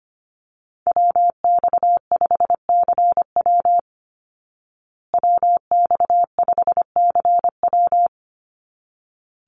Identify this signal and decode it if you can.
Morse code, heard as a keyed tone, about 25 wpm, 700 Hz. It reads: WX5CW WX5CW
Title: Call Signs Repeated 3x - 25wpm